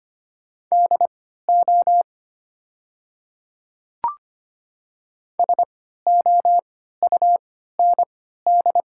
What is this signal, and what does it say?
Morse code, heard as a keyed tone, about 25 wpm, spaced as for 14 wpm, 700 Hz. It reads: DO SOUND
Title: 3x Character Spacing - Top 100 Words - 25wpm